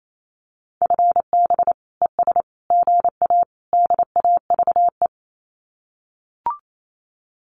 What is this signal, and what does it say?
Morse code, heard as a keyed tone, about 28 wpm, 700 Hz. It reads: FB ES GA DAVE
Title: Common QSO Phrases Encore 28wpm